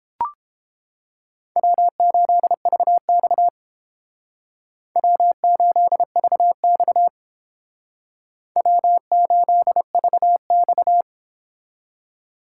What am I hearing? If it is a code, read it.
Morse code, heard as a keyed tone, about 30 wpm, 700 Hz. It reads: W8VX W8VX W8VX